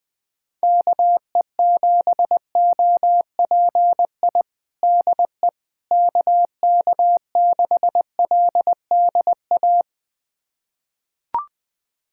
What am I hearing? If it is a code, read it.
Morse code, heard as a keyed tone, about 20 wpm, 700 Hz. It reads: KE7OPI DE KK6LDA
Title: USA Callsign DE Callsign - 6 Characters - 20wpm